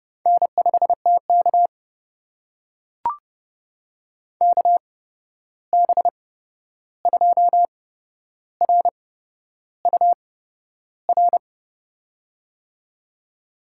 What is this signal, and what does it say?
Morse code, heard as a keyed tone, about 30 wpm, spaced as for 11 wpm, 700 Hz. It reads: N5TK KB2RUR
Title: USA Call Signs - 8x Character Spacing - 30wpm